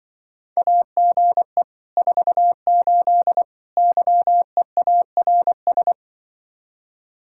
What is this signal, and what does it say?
Morse code, heard as a keyed tone, about 24 wpm, 700 Hz. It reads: AGE 48 YEARS
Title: Speed-Racing Common QSO Phrases Encore - 20wpm